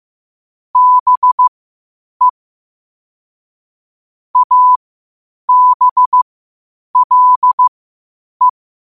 Morse code, heard as a keyed tone, about 15 wpm, spaced as for 8 wpm, 1000 Hz. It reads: BE ABLE
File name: qotdfiles / 8wpm